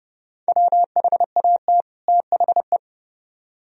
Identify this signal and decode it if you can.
Morse code, heard as a keyed tone, about 28 wpm, 700 Hz. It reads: WHAT THE